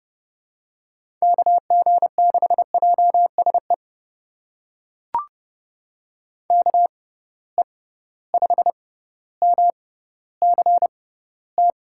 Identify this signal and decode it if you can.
Morse code, heard as a keyed tone, about 30 wpm, spaced as for 15 wpm, 700 Hz. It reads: KG6JSE KE5MCT
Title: USA Call Signs - 6 Characters - 6x Character Spacing - 30wpm